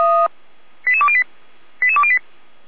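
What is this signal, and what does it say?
Five-tone selective call (ZVEI), heard as quick runs of five tones, tones of about 70 ms. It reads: 88208 88208
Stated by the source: ZVEI five-tone selective call 88305 received by SDR.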